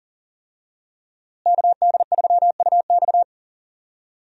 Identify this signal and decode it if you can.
Morse code, heard as a keyed tone, about 40 wpm, 700 Hz. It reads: KD3UX
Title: USA Call Signs - 5 Characters - Repeated 3x - 40wpm